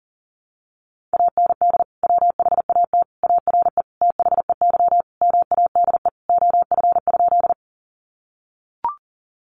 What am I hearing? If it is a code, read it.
Morse code, heard as a keyed tone, about 38 wpm, 700 Hz. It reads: AND WHAT ARE THEY MADE OF?
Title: Sentences from Top 200 Words 40wpm